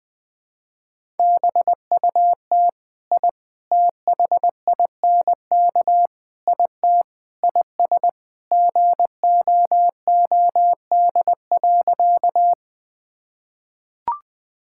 Morse code, heard as a keyed tone, about 21 wpm, 700 Hz. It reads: BUT I THINK IT IS GOOD.
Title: Sentences with 4-Word Phrases - Spoken Beginning - 20wpm